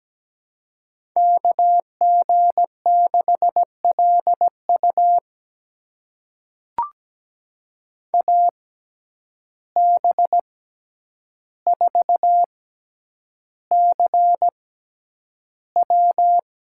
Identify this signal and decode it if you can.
Morse code, heard as a keyed tone, about 17 wpm, spaced as for 9 wpm, 700 Hz. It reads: KG6LU AB4CW
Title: USA Call Signs - 5 Characters - 6x Character Spacing - 17wpm